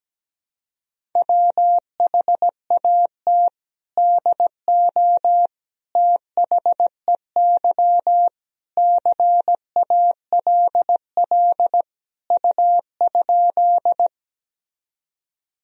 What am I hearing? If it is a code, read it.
Morse code, heard as a keyed tone, about 17 wpm, 700 Hz. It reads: WHAT DO THEY CALL U?
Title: Sentences from Top 100 Words and Abbreviations 17wpm